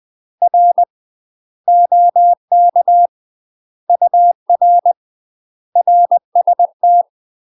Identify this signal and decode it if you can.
Morse code, heard as a keyed tone, about 20 wpm, 700 Hz. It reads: R OK UR RST
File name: UK MASTER QSOs TRAINING / 20WPM / UK MASTER QSOs 0073